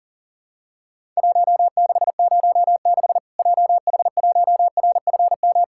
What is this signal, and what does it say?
Morse code, heard as a keyed tone, about 40 wpm, 700 Hz. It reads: 1606 JH1RFM
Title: Worked All Europe DX Contest - QTC Exchange - 40wpm